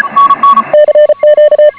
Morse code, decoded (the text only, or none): CQ